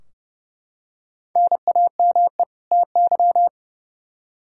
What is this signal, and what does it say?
Morse code, heard as a keyed tone, about 30 wpm, 700 Hz. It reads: NAME TY
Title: Practice QSO - Exchange 1 - Beginning - 30wpm